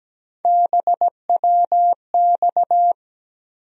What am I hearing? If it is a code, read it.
Morse code, heard as a keyed tone, about 17 wpm, 700 Hz. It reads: BWX
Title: USA Call Sign Postfixes - 3 Letters 17wpm